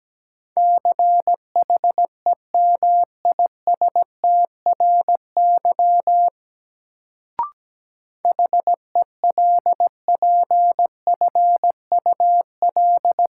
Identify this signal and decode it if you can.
Morse code, heard as a keyed tone, about 17 wpm, 700 Hz. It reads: CHEMISTRY HELPFUL
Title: Sentences from Top 1000 Words - Review Part 6 - New Word Drill - 17wpm